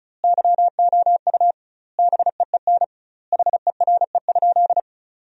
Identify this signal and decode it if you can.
Morse code, heard as a keyed tone, about 36 wpm, 700 Hz. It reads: YOU BEEN HERE?
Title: 2x Word-Spacing - Sentences from Top 200 Words - 35wpm